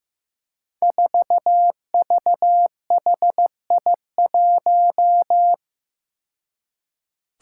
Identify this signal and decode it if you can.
Morse code, heard as a keyed tone, about 15 wpm, 700 Hz. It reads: 4VHI1
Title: Five Challenging Letter-Number Combinations 15wpm